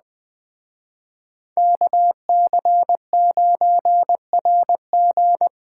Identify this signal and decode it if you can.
Morse code, heard as a keyed tone, about 21 wpm, 700 Hz. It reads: KC9RG
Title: Speed Racing - USA Call Signs 17wpm